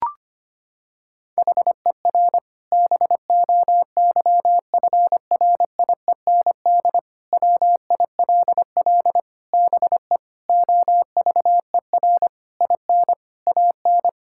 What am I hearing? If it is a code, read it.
Morse code, heard as a keyed tone, about 25 wpm, 700 Hz. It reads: HER BOYFRIEND WILL BE OVER IN AN
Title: Sentences from Top 1000 Words - Review Part 5 - 25wpm